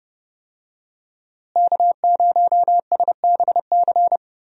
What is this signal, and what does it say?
Morse code, heard as a keyed tone, about 30 wpm, 700 Hz. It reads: K0SBC
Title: USA Call Signs - 5 Characters - Repeated 3x - 30wpm